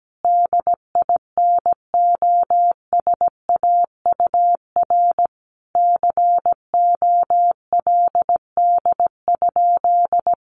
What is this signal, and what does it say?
Morse code, heard as a keyed tone, about 17 wpm, 700 Hz. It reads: DINOSAUR COLD?
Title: A Sound Of Thunder 17wpm